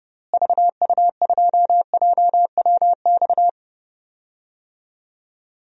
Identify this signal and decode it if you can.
Morse code, heard as a keyed tone, about 30 wpm, 700 Hz. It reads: VU2JWX
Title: DXCC-100 Call Signs - 6 Characters - Repeated 2x - 30wpm